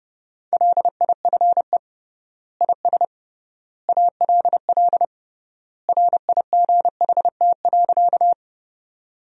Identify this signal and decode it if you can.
Morse code, heard as a keyed tone, about 30 wpm, 700 Hz. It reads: LIFE IS ALL RIGHT.
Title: 3x Word-Spacing - Sentences from Top 300 Words - 30wpm